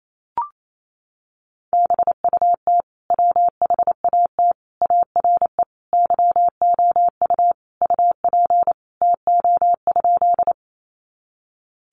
Morse code, heard as a keyed tone, about 28 wpm, 700 Hz. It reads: BUT WHAT ARE YOU UP TO?
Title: Sentences from Top 100 Words 28wpm